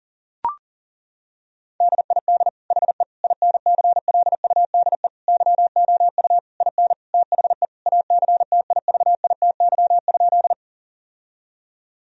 Morse code, heard as a keyed tone, about 37 wpm, 700 Hz. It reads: DID HE INCLUDE YOU IN THE ACTIVITY?
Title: Sentences from Top 1000 Words - Review Part 4 - 40wpm